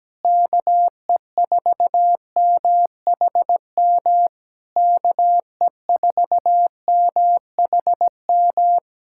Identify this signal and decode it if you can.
Morse code, heard as a keyed tone, about 17 wpm, 700 Hz. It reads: KE4MHM KE4MHM
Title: Speed-Racing Common QSO Phrases Encore - 17wpm